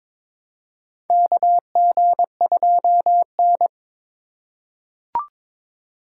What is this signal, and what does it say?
Morse code, heard as a keyed tone, about 22 wpm, 700 Hz. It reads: KG2N
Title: USA Call Signs - 4 Characters - 3x Character Spacing - 22wpm